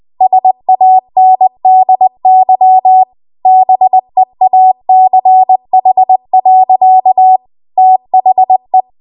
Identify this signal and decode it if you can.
Morse code, heard as a keyed tone, about 20 wpm, 750 Hz. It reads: SANDY BEACH. THE
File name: Morse / 20WPM